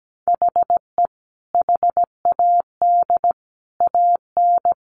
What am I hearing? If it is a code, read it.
Morse code, heard as a keyed tone, about 17 wpm, 700 Hz. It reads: HE HAD AN
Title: Sentences from Top 800 Words 17wpm